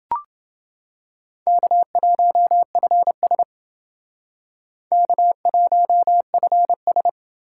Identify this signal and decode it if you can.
Morse code, heard as a keyed tone, about 28 wpm, 700 Hz. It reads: K1FS K1FS